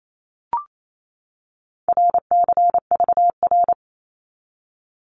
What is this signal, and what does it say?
Morse code, heard as a keyed tone, about 28 wpm, 700 Hz. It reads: RCVR
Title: Abbreviations 28wpm